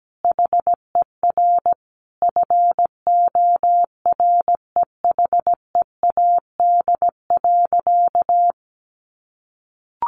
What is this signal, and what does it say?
Morse code, heard as a keyed tone, about 17 wpm, 700 Hz. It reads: HER FOREHEAD.
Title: Easy Phrases 17wpm